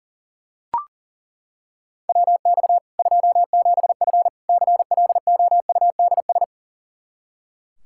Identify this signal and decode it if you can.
Morse code, heard as a keyed tone, about 40 wpm, 700 Hz. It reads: WX 27F CLOUDS
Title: K1USN SST QSO Practice 40wpm